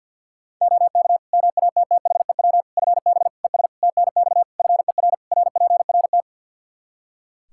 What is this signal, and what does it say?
Morse code, heard as a keyed tone, about 50 wpm, 700 Hz. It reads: OK MATTHEW FB ES TNX FER RPRT
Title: Practice QSO - Exchange 2 - Info - 50wpm